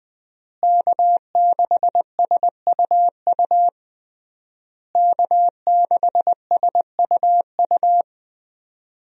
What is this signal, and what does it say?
Morse code, heard as a keyed tone, about 20 wpm, 700 Hz. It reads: K6SUU K6SUU
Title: USA Call Signs - 5 Characters - Repeated 3x - 20wpm